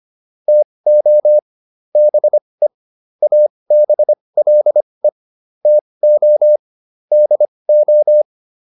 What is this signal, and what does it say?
Morse code, heard as a keyed tone, about 25 wpm, spaced as for 20 wpm, 600 Hz. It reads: TO BE ABLE TO DO